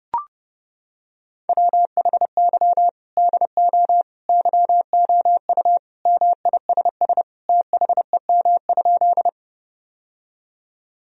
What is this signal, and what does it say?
Morse code, heard as a keyed tone, about 28 wpm, 700 Hz. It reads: WHY DO YOU MISS THEM?
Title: Sentences from Top 1000 Words - Review Part 3 - 30wpm